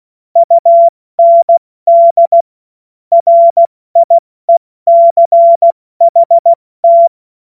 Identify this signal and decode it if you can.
Morse code, heard as a keyed tone, about 16 wpm, 680 Hz. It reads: UND RIECHT
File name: Morsekurs / Witze / Witz 14. Juni 2022 70 F80